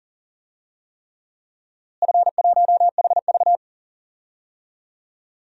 Morse code, heard as a keyed tone, about 40 wpm, 700 Hz. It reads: F1HV
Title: DXCC-100 Call Signs - 4 Characters - Repeated 3x - 40wpm